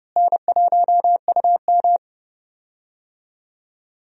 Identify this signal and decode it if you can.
Morse code, heard as a keyed tone, about 30 wpm, 700 Hz. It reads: N1UM
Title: USA Call Signs - 4 Characters - Repeated 2x - 30wpm